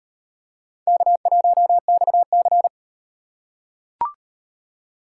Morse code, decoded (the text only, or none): K1XC